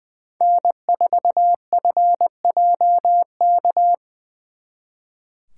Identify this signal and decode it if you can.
Morse code, heard as a keyed tone, about 24 wpm, 700 Hz. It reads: N4FJK
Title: Speed Racing - USA 1x3 Call Signs 20wpm